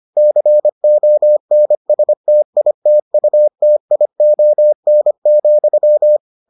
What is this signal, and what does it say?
Morse code, heard as a keyed tone, about 25 wpm, 600 Hz. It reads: CONSTITUTION,